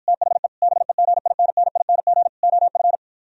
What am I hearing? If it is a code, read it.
Morse code, heard as a keyed tone, about 53 wpm, 700 Hz. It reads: THE BEGINNING OF